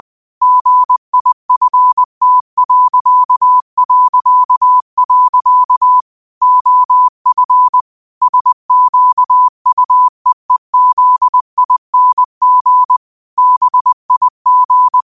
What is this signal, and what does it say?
Morse code, heard as a keyed tone, about 20 wpm, 1000 Hz. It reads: GIFT... OF SQUEEZING BIG
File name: qotdfiles / 20wpm